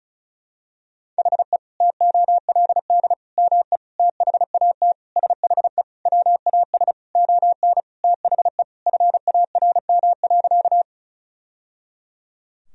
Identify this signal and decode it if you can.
Morse code, heard as a keyed tone, about 34 wpm, 700 Hz. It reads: HE TOLD ME THAT SHE WAS ON THE FARM.
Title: Sentences from Top 1000 Words - Review Part 2 - 35wpm